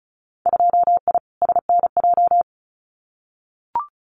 Morse code, heard as a keyed tone, about 36 wpm, 700 Hz. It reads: 2I SNJ
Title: Winter Field Day Exchanges Encore 35wpm